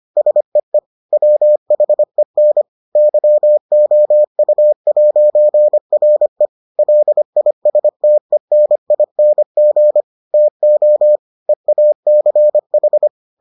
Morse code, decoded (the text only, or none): SEE WHEN YOU'RE LISTENING TO EACH